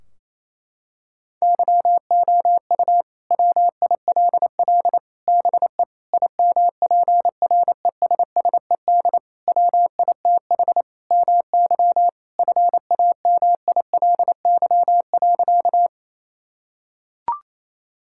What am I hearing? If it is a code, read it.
Morse code, heard as a keyed tone, about 28 wpm, 700 Hz. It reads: YOU WILL BE IMPRESSED WITH MY FAMILY.
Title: Sentences from Top 1600 Words 28wpm